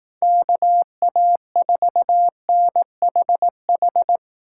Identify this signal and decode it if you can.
Morse code, heard as a keyed tone, about 19 wpm, 700 Hz. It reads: KA4NHH
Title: Speed Racing - USA 2x3 Call Signs 15wpm